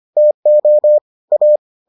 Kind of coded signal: Morse code

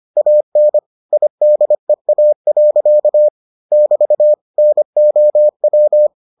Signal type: Morse code

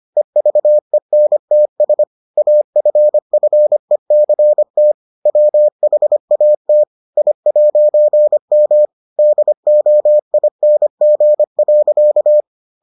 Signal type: Morse code

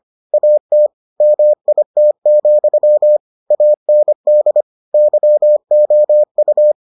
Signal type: Morse code

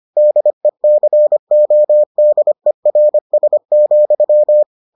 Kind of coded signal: Morse code

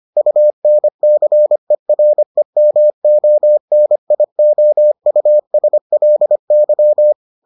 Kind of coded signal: Morse code

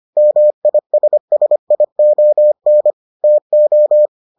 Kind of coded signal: Morse code

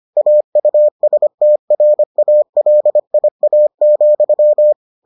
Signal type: Morse code

